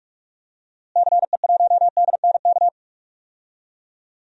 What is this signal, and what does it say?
Morse code, heard as a keyed tone, about 46 wpm, 700 Hz. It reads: CE1DNK
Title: DXCC-100 Call Signs - 6 Characters - Repeated 3x - 45wpm